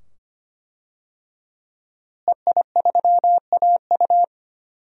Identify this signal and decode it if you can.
Morse code, heard as a keyed tone, about 25 wpm, 700 Hz. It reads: EI3AU
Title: DXCC-100 Call Signs - 5 Characters - Repeated 2x - 25wpm